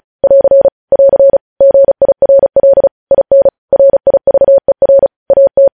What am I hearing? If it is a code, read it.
Morse code, heard as a keyed tone, about 36 wpm, 550 Hz. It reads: <AR> <AR> GIRL IN RIVER AT
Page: News Headlines in Morse Code at 35 WPM